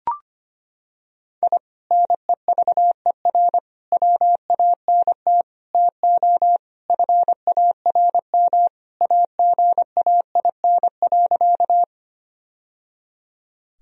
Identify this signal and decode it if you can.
Morse code, heard as a keyed tone, about 25 wpm, 700 Hz. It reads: I NEVER WANT TO FARM AGAIN.
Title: Sentences from Top 1000 Words - Review Part 2 - 25wpm